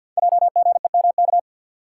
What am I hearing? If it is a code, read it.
Morse code, heard as a keyed tone, about 50 wpm, 700 Hz. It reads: JGEMK
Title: Code Groups - 5 Letters - 50wpm